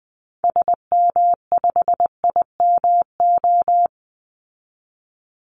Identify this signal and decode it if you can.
Morse code, heard as a keyed tone, about 20 wpm, 700 Hz. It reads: SM5IMO
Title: Call Signs Repeated 3x - 20wpm